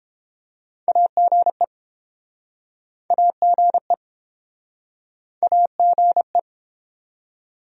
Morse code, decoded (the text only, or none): AGE AGE AGE